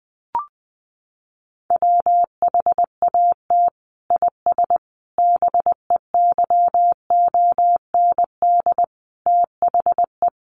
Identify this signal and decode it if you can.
Morse code, heard as a keyed tone, about 20 wpm, 700 Hz. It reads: WHAT IS BEYOND THE
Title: Sentences from Top 1100 Words 20wpm